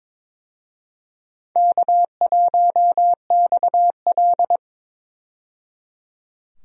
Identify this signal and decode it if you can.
Morse code, heard as a keyed tone, about 22 wpm, 700 Hz. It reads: K1XL
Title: USA Call Signs - 4 Characters - 22wpm